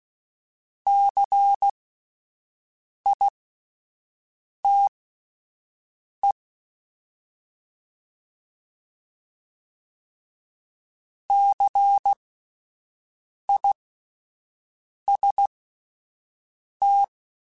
Morse code, heard as a keyed tone, about 16 wpm, spaced as for 5 wpm, 800 Hz. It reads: CITE CIST
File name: Spaced Alphas / Spaced Alphas mp3s / Cs sp @10